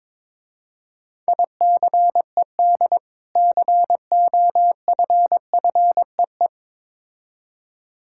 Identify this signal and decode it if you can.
Morse code, heard as a keyed tone, about 22 wpm, 700 Hz. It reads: ICED COFFEE
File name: Difficult Words with Hints - 22wpm